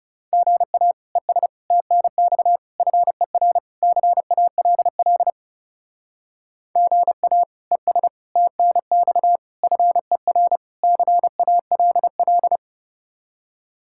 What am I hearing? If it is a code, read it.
Morse code, heard as a keyed tone, about 36 wpm, 700 Hz. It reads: GA ES TNX FER CALL GA ES TNX FER CALL
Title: Speed-Racing Common QSO Phrases - 30wpm